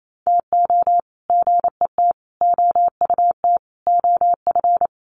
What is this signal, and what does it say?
Morse code, heard as a keyed tone, about 28 wpm, 700 Hz. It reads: TO GET OUT OF
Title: Sets of 5 Words Encore2 28wpm